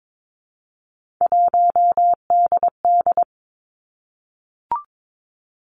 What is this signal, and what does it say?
Morse code, heard as a keyed tone, about 22 wpm, 700 Hz. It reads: 1DD
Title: Three Challenging Letter-Number Combinations 22wpm